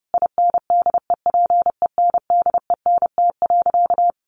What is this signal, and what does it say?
Morse code, heard as a keyed tone, about 30 wpm, 700 Hz. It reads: INDEPENDENT.
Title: Sentences from Top 900 Words 30wpm